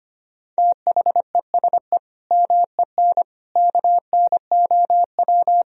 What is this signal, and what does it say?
Morse code, heard as a keyed tone, about 25 wpm, 700 Hz. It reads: THESE MEN KNOW
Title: Sentences from Top 1000 Words - Review Part 1 - 25wpm